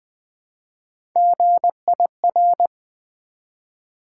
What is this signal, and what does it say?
Morse code, heard as a keyed tone, about 20 wpm, 700 Hz. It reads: GIR
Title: Build A Word - 1 Letter At A Time - 4-Letter Words - 20wpm